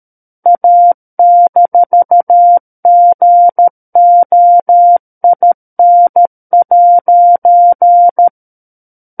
Morse code, heard as a keyed tone, about 13 wpm, 700 Hz. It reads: A-GOIN'
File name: Book One / 04-Chapter 4 - 13 WPM